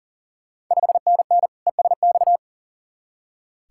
Morse code, heard as a keyed tone, about 40 wpm, 690 Hz. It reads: 5NN ESX
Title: Ontario QSO Party Exchange 40wpm